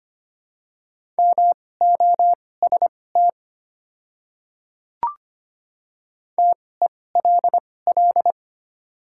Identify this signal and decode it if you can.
Morse code, heard as a keyed tone, about 25 wpm, spaced as for 18 wpm, 700 Hz. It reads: MOST TELL
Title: 2x Character Spacing - Top 300 Words - 25wpm